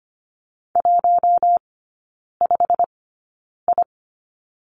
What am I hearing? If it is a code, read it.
Morse code, heard as a keyed tone, about 25 wpm, spaced as for 9 wpm, 690 Hz. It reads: 15I
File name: Lesson 15e - Number 5 - Three Characters 25wpm